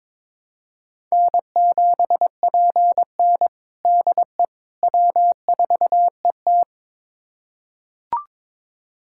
Morse code, heard as a keyed tone, about 22 wpm, 700 Hz. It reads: N7PN DE W4ET
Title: USA Callsign DE Callsign - 4 Characters - 22wpm